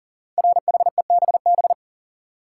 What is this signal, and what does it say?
Morse code, heard as a keyed tone, about 43 wpm, 700 Hz. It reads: RHEBB
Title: Code Groups - 5 Letters - 40wpm